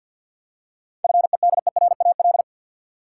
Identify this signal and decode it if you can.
Morse code, heard as a keyed tone, about 53 wpm, 700 Hz. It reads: FEDERAL
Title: Seven Letter Words - 50wpm